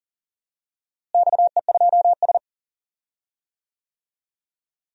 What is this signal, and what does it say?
Morse code, heard as a keyed tone, about 40 wpm, 700 Hz. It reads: XE2S
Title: DXCC-100 Call Signs - 4 Characters - Repeated 2x - 40wpm